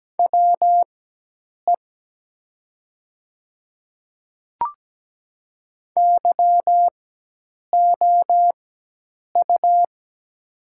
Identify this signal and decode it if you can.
Morse code, heard as a keyed tone, about 17 wpm, spaced as for 8 wpm, 700 Hz. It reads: WE YOU
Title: 4x Character Spacing - Top 100 Words - 17wpm